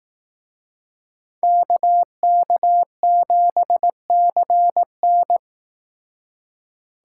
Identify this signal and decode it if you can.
Morse code, heard as a keyed tone, about 19 wpm, 700 Hz. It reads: KK7CN
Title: Speed Racing - USA 2x2 Call Signs 15wpm